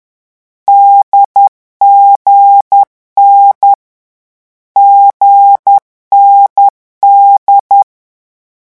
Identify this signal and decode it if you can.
Morse code, heard as a keyed tone, about 10 wpm, 800 Hz. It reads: DGN GND
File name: morse mp3 / lecon4-vitesse 10